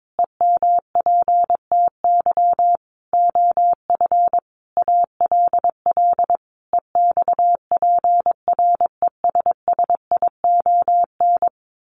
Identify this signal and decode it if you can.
Morse code, heard as a keyed tone, about 22 wpm, 700 Hz. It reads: EMPTY OF ALL EXPRESSION
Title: A Sound Of Thunder 22wpm